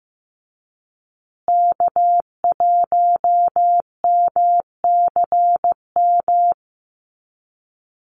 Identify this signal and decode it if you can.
Morse code, heard as a keyed tone, about 15 wpm, 700 Hz. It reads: K1MCM
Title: USA Call Signs Repeated 3x 15wpm